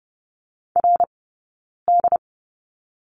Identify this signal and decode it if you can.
Morse code, heard as a keyed tone, about 30 wpm, spaced as for 9 wpm, 690 Hz. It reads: RD
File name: Lesson 17d - Letter u - Two Characters 30wpm